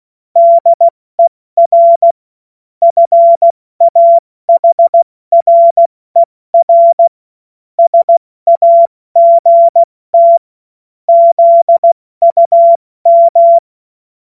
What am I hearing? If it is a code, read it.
Morse code, heard as a keyed tone, about 16 wpm, 680 Hz. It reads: DER FAHRER SAGT ZUM
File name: Morsekurs / Witze / Witz 20. September 2022 70 F80